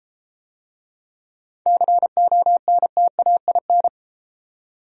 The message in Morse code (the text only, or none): CONTAIN